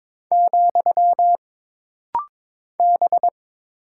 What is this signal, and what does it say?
Morse code, heard as a keyed tone, about 22 wpm, 700 Hz. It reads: , B
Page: This rapid-fire timing is geared to push your ICR (Instant Character Recognition) skills.